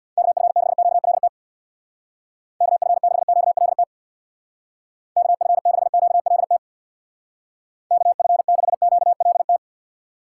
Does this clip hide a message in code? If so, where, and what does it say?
Yes: Morse code from 0.2 to 9.6 s, heard as a keyed tone, about 64 wpm, 700 Hz. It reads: KF6QLT KF6QLT KF6QLT KF6QLT